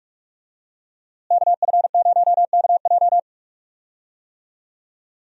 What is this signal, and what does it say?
Morse code, heard as a keyed tone, about 46 wpm, 700 Hz. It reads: KF0KJ
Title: USA Call Signs - 5 Characters - Repeated 2x - 45wpm